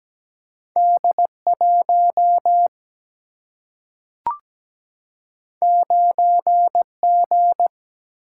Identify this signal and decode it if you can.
Morse code, heard as a keyed tone, about 17 wpm, 700 Hz. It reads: D1 9G
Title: Single-Letter Single-Number 17wpm